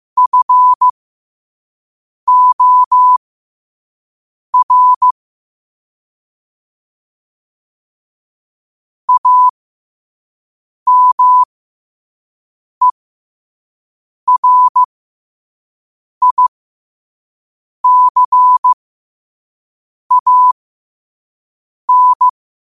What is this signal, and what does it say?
Morse code, heard as a keyed tone, about 15 wpm, spaced as for 5 wpm, 1000 Hz. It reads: FOR AMERICAN
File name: qotdfiles / 5wpm